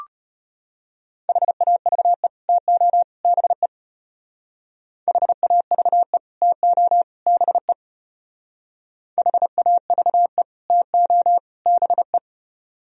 Morse code, heard as a keyed tone, about 33 wpm, 700 Hz. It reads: HAVE TO BE HAVE TO BE HAVE TO BE